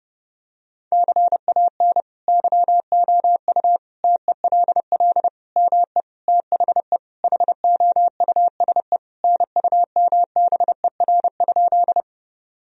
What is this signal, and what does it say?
Morse code, heard as a keyed tone, about 30 wpm, 700 Hz. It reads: CAN YOU TELL ME THE HOUSE NUMBER?
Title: Sentences from Top 1000 Words - Review Part 1 - 30wpm